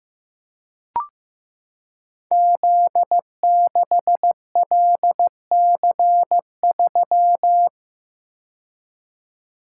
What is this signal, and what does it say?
Morse code, heard as a keyed tone, about 15 wpm, 700 Hz. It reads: Z6LC3
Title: Code Groups - 5 Letter:Numbers - 15wpm